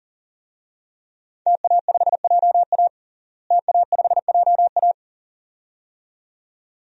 Morse code, heard as a keyed tone, about 37 wpm, 700 Hz. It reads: TA5JA TA5JA
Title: DXCC-100 Call Signs - 5 Characters - Repeated 2x - 40wpm